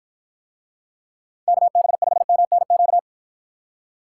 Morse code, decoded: KB5MNQ

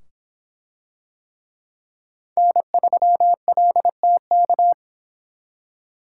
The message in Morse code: N3LTK

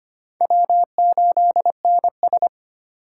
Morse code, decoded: W8NS

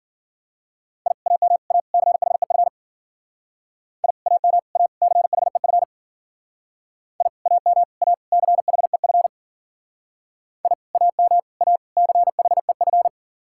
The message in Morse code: I AM A CHEF I AM A CHEF I AM A CHEF I AM A CHEF

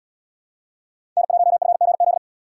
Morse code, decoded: N1FGZ

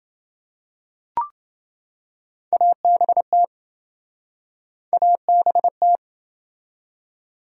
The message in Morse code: ABT ABT

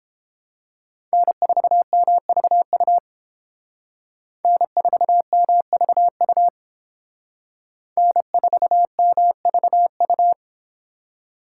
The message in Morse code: N4MVU N4MVU N4MVU

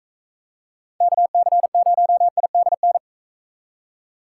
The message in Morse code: KC0IDN

